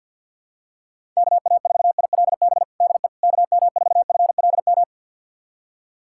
Morse code, heard as a keyed tone, about 51 wpm, 700 Hz. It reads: KA4ILB DE KM4FLM
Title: USA Callsign DE Callsign 50wpm